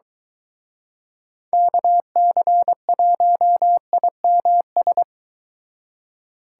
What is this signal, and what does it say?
Morse code, heard as a keyed tone, about 24 wpm, 700 Hz. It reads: KC1IMS